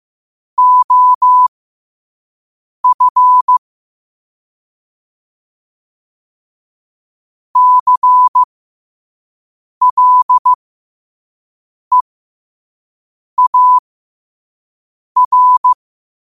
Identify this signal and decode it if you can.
Morse code, heard as a keyed tone, about 15 wpm, spaced as for 5 wpm, 1000 Hz. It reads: OF CLEAR
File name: qotdfiles / 5wpm